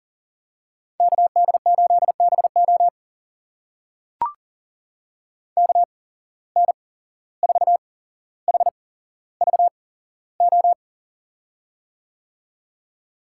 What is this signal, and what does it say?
Morse code, heard as a keyed tone, about 40 wpm, spaced as for 16 wpm, 700 Hz. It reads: KD8BO KN4HVO
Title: USA Call Signs - 8x Character Spacing - 40wpm